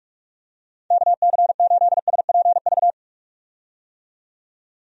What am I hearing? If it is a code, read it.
Morse code, heard as a keyed tone, about 46 wpm, 700 Hz. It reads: KC8SPV
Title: USA Call Signs - 6 Characters - 45wpm